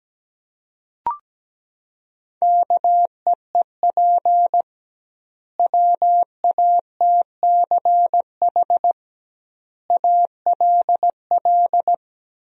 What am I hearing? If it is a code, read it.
Morse code, heard as a keyed tone, about 17 wpm, 700 Hz. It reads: KEEP WATCH ALL
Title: 2x Word-Spacing - Sentences from Top 300 Words - 17wpm